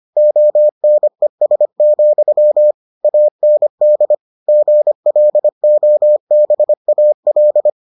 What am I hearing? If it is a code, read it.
Morse code, heard as a keyed tone, about 25 wpm, 600 Hz. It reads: ONES, AND GLOBAL